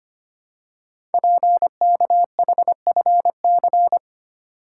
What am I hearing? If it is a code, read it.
Morse code, heard as a keyed tone, about 25 wpm, 700 Hz. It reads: PKHFC
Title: Code Groups - 5 Letters - 25wpm